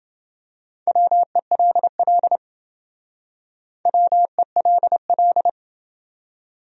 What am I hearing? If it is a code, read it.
Morse code, heard as a keyed tone, about 28 wpm, 700 Hz. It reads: WELL WELL